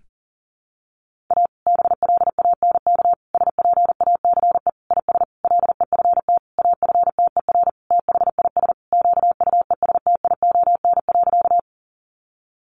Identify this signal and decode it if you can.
Morse code, heard as a keyed tone, about 40 wpm, 700 Hz. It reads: A BLANK SPACE IS LEFT AFTER THIS QUESTION.
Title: Sentences from Top 1500 Words 40wpm